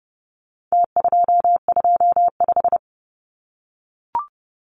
Morse code, heard as a keyed tone, about 30 wpm, 700 Hz. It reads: T225
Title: Numbers - Contest Style - 4-Digit 30wpm